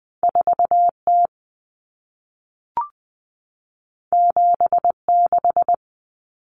Morse code, heard as a keyed tone, about 20 wpm, 700 Hz. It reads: 4T 76
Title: Common Cut Numbers - 2-Digit 20wpm